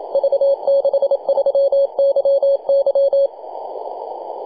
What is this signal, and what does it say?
Morse code, heard as a keyed tone, about 28 wpm, 550 Hz. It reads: V63YY
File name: posts / 2015-nov-21-0614z-v63yy-24-905mhz-cw